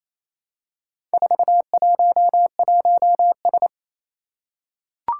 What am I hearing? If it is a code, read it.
Morse code, heard as a keyed tone, about 27 wpm, 700 Hz. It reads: 411S
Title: Four Challenging Letter-Number Combinations 28wpm